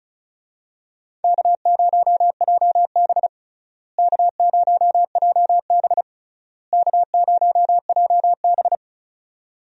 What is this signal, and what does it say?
Morse code, heard as a keyed tone, about 36 wpm, 700 Hz. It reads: K0JB K0JB K0JB
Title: USA Call Signs - 4 Characters - Repeated 3x - 35wpm